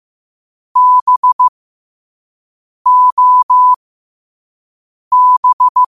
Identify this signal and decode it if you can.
Morse code, heard as a keyed tone, about 15 wpm, spaced as for 5 wpm, 1000 Hz. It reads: BOB
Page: qotd-at-5-wpm